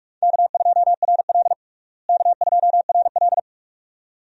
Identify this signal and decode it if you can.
Morse code, heard as a keyed tone, about 42 wpm, 700 Hz. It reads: K2RL K2RL
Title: USA Call Signs - 4 Characters - Repeated 2x - 45wpm